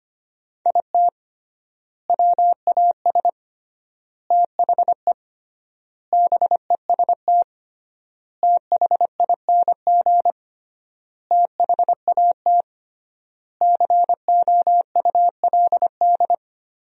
Morse code, heard as a keyed tone, about 25 wpm, 700 Hz. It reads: IT WAS THE BEST THING THAT COULD
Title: 3x Word-Spacing - Sentences from Top 400 Words - 25wpm